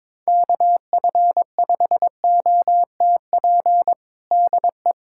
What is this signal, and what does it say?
Morse code, heard as a keyed tone, about 22 wpm, 700 Hz. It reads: KF5OTP DE
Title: USA Callsign DE Callsign 22wpm